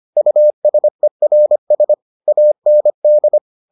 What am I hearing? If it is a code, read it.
Morse code, heard as a keyed tone, about 25 wpm, 600 Hz. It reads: USERS AND